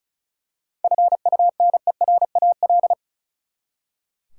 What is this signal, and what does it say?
Morse code, heard as a keyed tone, about 36 wpm, 700 Hz. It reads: FUNERAL
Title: Seven Letter Words - 35wpm